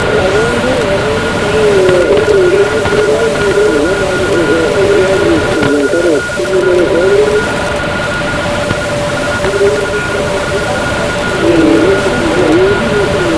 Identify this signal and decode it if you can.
Two-tone teleprinter signal, about 850 Hz shift.